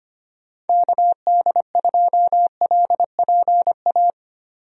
Morse code, heard as a keyed tone, about 25 wpm, 700 Hz. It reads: KD2LPA
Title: USA Call Signs - 6 Characters - 25wpm